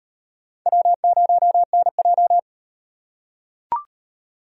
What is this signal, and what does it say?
Morse code, heard as a keyed tone, about 37 wpm, 700 Hz. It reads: W0NJ